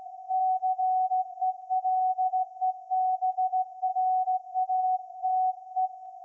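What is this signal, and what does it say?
Morse code, heard as a keyed tone, about 16 wpm, 750 Hz. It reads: CELEBRATE
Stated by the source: Morse code from SAQ, received at a station in Putignano, Apulia, Italy, on VLF.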